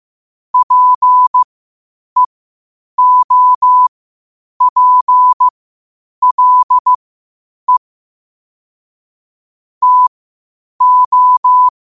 Morse code, heard as a keyed tone, about 15 wpm, spaced as for 8 wpm, 1000 Hz. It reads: PEOPLE TO
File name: qotdfiles / 8wpm